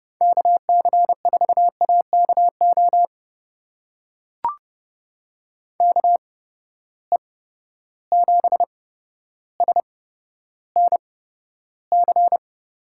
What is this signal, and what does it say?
Morse code, heard as a keyed tone, about 32 wpm, spaced as for 13 wpm, 700 Hz. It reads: KC4AKO KE7SNC
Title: USA Call Signs - 6 Characters - 8x Character Spacing - 30wpm